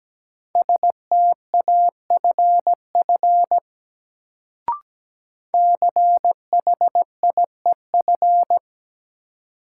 Morse code, heard as a keyed tone, about 17 wpm, 700 Hz. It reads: STAFF CHIEF
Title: Sets of 5 Words Encore - New Word Drill - 17wpm